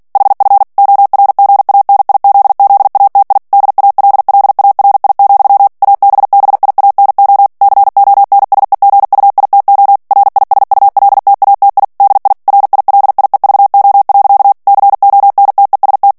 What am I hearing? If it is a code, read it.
Morse code, heard as a keyed tone, about 48 wpm, 800 Hz. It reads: HF ORGANIZZATI DALLARI, ABBIANO CONSEGUITO RISULTATI DI RILIEVO. CONTEST
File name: ITALIEN texte clair 2 vitesse 45 mots minute